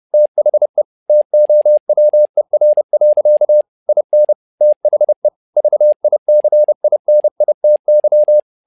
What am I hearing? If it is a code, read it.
Morse code, heard as a keyed tone, about 30 wpm, 600 Hz. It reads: THE TOWER. IN THE VICINITY
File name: learning center / tower trouble / tower trouble 30wpm